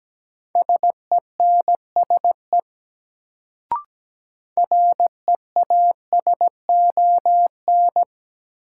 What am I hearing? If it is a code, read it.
Morse code, heard as a keyed tone, about 17 wpm, 700 Hz. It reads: SENSE REASON
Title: Sentences from Top 700 Words - New Word Drill - 17wpm